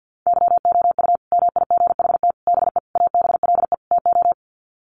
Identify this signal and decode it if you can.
Morse code, heard as a keyed tone, about 50 wpm, 700 Hz. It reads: YOU MIGHT BE ABLE TO
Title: Sets of 5 Words Encore2 50wpm